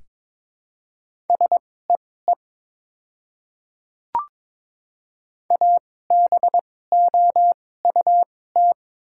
Morse code, heard as a keyed tone, about 22 wpm, spaced as for 16 wpm, 700 Hz. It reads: SEE ABOUT
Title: 2x Character Spacing - Top 100 Words - 22wpm